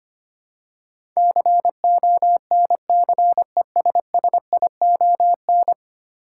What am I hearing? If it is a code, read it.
Morse code, heard as a keyed tone, about 25 wpm, 700 Hz. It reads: CONCESSION
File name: Ten Letter Words - 25wpm